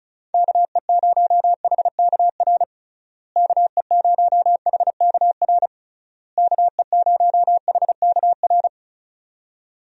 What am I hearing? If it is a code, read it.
Morse code, heard as a keyed tone, about 34 wpm, 700 Hz. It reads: KE0HKR KE0HKR KE0HKR
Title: USA Call Signs - 6 Characters - Repeated 3x - 35wpm